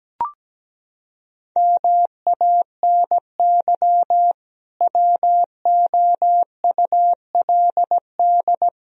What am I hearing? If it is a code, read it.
Morse code, heard as a keyed tone, about 17 wpm, 700 Hz. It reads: MANY WOULD
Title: Sentences from Top 1000 Words - Review Part 1 - 17wpm